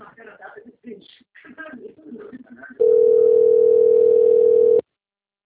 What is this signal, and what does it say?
Ringback tone (North America), 440 + 480 Hz, with speech.
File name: out- - - - - .40115